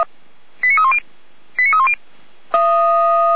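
Five-tone selective call (ZVEI), heard as quick runs of five tones, tones of about 70 ms. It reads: 89310 89310
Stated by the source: Received by SDR; ZVEI five-tone call.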